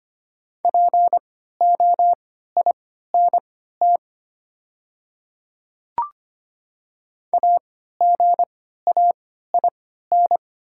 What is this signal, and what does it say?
Morse code, heard as a keyed tone, about 25 wpm, spaced as for 14 wpm, 700 Hz. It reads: POINT AGAIN
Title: 3x Character Spacing - Top 200 Words - 25wpm